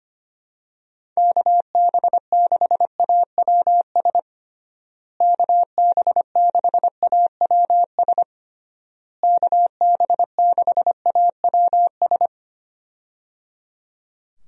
Morse code, heard as a keyed tone, about 25 wpm, 700 Hz. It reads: KB6AWS KB6AWS KB6AWS
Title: USA Call Signs - 6 Characters - Repeated 3x - 25wpm